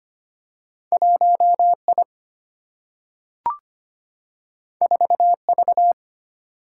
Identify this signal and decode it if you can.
Morse code, heard as a keyed tone, about 25 wpm, 700 Hz. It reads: 1I 4V
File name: Two Letter-Number Combinations - 25wpm